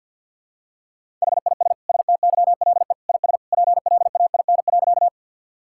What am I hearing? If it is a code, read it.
Morse code, heard as a keyed tone, about 53 wpm, 700 Hz. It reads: HIS STYLE IS PLAIN.
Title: Sentences from Top 1000 Words - Review Part 5 - 50wpm